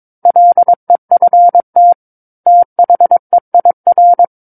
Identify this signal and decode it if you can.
Morse code, heard as a keyed tone, about 22 wpm, 700 Hz. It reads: LEFT THEIR
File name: Book One / 13-Chapter 13 - 22 WPM